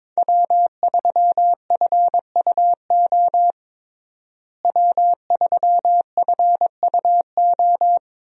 Morse code, heard as a keyed tone, about 22 wpm, 700 Hz. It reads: W3FUO W3FUO
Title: USA Call Signs - 5 Characters - Repeated 3x - 22wpm